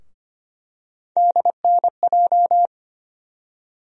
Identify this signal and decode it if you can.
Morse code, heard as a keyed tone, about 25 wpm, 700 Hz. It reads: DNJ
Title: USA Call Sign Postfixes - 3 Letters 25wpm